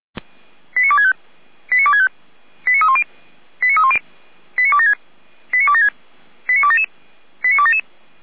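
Five-tone selective call (ZVEI), heard as quick runs of five tones, tones of about 70 ms. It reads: 89265 89265 89310 89310 89276 89276 89277 89277
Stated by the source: Received on an SDR; ZVEI five-tone alert from the server of Feuerwehr Orlamünde (f85360).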